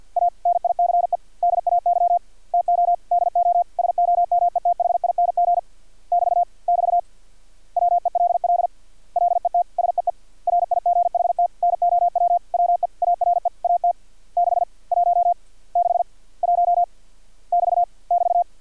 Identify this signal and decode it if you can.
Morse code, heard as a keyed tone, about 50 wpm, 700 Hz. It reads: A NICE DAY TO DO SOMETHING <BT> <BT> WELL LET SEE RIGHT NOW WE ARE AT 6 1 6 1 <BT> <BT>
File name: archived / CRSnet